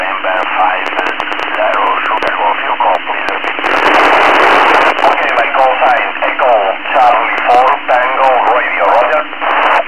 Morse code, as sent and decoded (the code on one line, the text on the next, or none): none